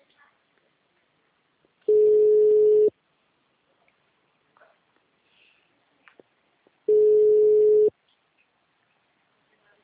Ringback tone (Europe, 425 Hz).
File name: out- -2009- - - .501